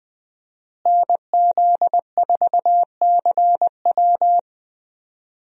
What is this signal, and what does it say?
Morse code, heard as a keyed tone, about 20 wpm, 700 Hz. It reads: NZ4CW
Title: USA Call Signs - 5 Characters - Repeated 2x - 20wpm